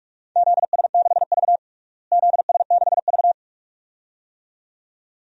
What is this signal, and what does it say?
Morse code, heard as a keyed tone, about 46 wpm, 700 Hz. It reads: ZS6V ZS6V
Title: DXCC-100 Call Signs - 4 Characters - Repeated 2x - 45wpm